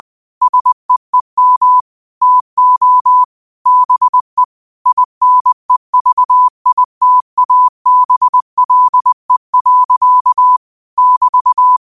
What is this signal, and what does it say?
Morse code, heard as a keyed tone, about 20 wpm, 1000 Hz. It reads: SEEM TO BE INEVITABLE. <BT>
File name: qotdfiles / 20wpm